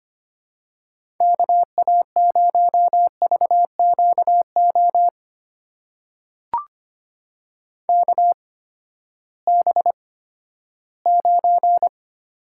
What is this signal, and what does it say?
Morse code, heard as a keyed tone, about 25 wpm, spaced as for 14 wpm, 700 Hz. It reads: KA0VQO KB9
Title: USA Call Signs - 6 Characters - 8x Character Spacing - 25wpm